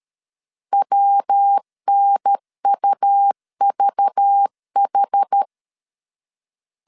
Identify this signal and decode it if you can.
Morse code, heard as a keyed tone, about 13 wpm, 800 Hz. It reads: WNUVH
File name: CW / DET / 1 6 1500C S A 12wpm 3440